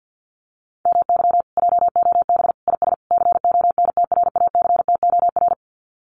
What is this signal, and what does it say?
Morse code, heard as a keyed tone, about 48 wpm, 700 Hz. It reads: MY JOB IS CONTRACTOR
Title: Common QSO Phrases Encore 50wpm